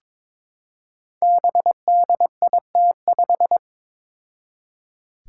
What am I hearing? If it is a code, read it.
Morse code, heard as a keyed tone, about 22 wpm, 700 Hz. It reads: BDIT5
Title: Five Challenging Letter-Number Combinations 22wpm